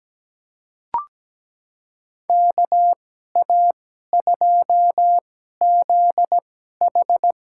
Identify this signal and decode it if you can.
Morse code, heard as a keyed tone, about 17 wpm, spaced as for 12 wpm, 700 Hz. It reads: KA2ZH
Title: USA Call Signs - 2x Character Spacing - 17wpm